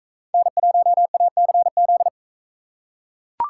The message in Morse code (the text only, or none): N1ACZ